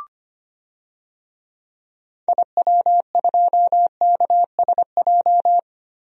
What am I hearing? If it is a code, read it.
Morse code, heard as a keyed tone, about 25 wpm, 700 Hz. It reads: IW2KSJ
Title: DXCC-100 Call Signs - 6 Characters - Repeated 2x - 25wpm